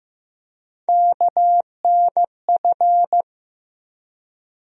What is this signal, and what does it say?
Morse code, heard as a keyed tone, about 15 wpm, 700 Hz. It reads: KNF
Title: USA Call Sign Postfixes - 3 Letters 15wpm